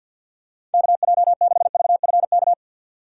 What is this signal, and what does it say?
Morse code, heard as a keyed tone, about 46 wpm, 700 Hz. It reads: KJ6VFK
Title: USA Call Signs - 6 Characters - 4x Character Spacing - 50wpm